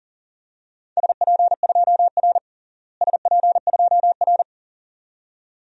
Morse code, heard as a keyed tone, about 37 wpm, 700 Hz. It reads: SP2R SP2R
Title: DXCC-100 Call Signs - 4 Characters - Repeated 2x - 40wpm